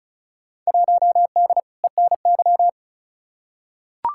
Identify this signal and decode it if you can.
Morse code, heard as a keyed tone, about 36 wpm, 700 Hz. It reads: 1D ENY
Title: Field Day Exchanges - 35wpm